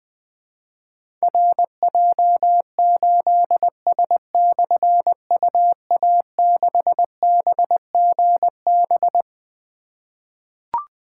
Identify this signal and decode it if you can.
Morse code, heard as a keyed tone, about 20 wpm, 700 Hz. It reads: RJ8S/UA6BGB
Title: Difficult Call Signs 20wpm